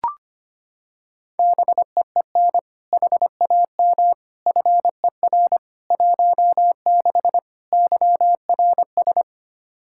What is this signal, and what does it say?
Morse code, heard as a keyed tone, about 25 wpm, 700 Hz. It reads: BEEN HAM FER 16 YRS
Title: Practice QSO - Exchange 3 - More Info - 25wpm